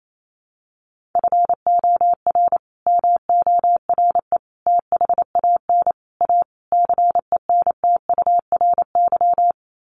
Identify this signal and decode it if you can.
Morse code, heard as a keyed tone, about 28 wpm, 700 Hz. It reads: FOR MORE THAN A CENTURY
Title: Sets of 5 Words Encore2 28wpm